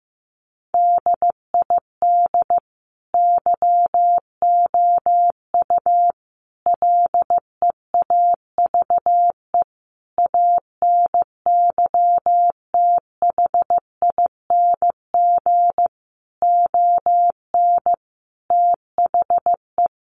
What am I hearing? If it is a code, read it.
Morse code, heard as a keyed tone, about 15 wpm, 700 Hz. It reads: DID YOU LEAVE ANYTHING ON THE
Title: Sentences from Top 1000 Words 15wpm